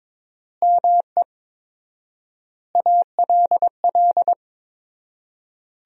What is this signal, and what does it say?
Morse code, heard as a keyed tone, about 22 wpm, 700 Hz. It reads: ME ALL
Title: 4x Word-Spacing - Sentences from Top 400 Words - 22wpm